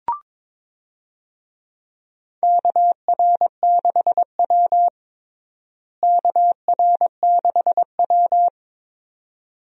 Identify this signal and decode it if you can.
Morse code, heard as a keyed tone, about 22 wpm, 700 Hz. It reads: KR6W KR6W
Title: USA Call Signs - 4 Characters - Repeated 2x - 22wpm